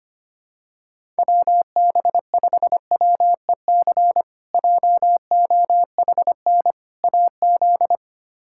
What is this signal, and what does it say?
Morse code, heard as a keyed tone, about 25 wpm, 700 Hz. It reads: WB5WEC JOHN AZ
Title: K1USN SST QSO Practice - 25wpm